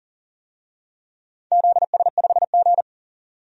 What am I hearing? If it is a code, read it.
Morse code, heard as a keyed tone, about 41 wpm, 700 Hz. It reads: ZS5G
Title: DXCC-100 Call Signs - 4 Characters - 40wpm